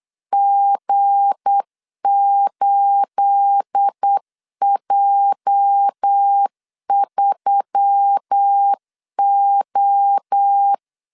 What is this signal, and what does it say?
Morse code, heard as a keyed tone, about 8 wpm, 800 Hz. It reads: G8J3O
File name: LEZIONI / 9 28 8wpm 3512